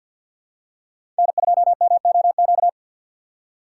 Morse code, heard as a keyed tone, about 48 wpm, 700 Hz. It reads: N2MOQ
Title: USA Call Signs - 5 Characters - 6x Character Spacing - 50wpm